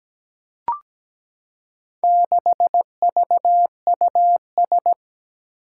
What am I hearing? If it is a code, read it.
Morse code, heard as a keyed tone, about 17 wpm, 700 Hz. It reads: 6VUS
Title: Four Challenging Letter-Number Combinations 17wpm